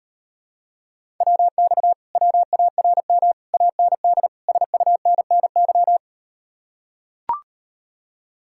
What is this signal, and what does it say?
Morse code, heard as a keyed tone, about 39 wpm, 700 Hz. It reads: WX WARM AND SUNNY